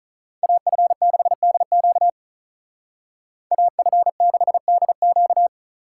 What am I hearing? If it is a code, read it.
Morse code, heard as a keyed tone, about 44 wpm, 700 Hz. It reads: AF6DQ AF6DQ
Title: Speed Racing - USA 2x2 Call Signs 35wpm